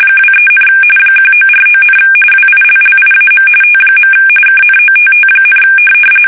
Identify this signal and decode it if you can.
RTTY, about 45.45 baud. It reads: 23, 1949, WILLIAM T. KNOTT,
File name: fsk850 / teletype-RTTY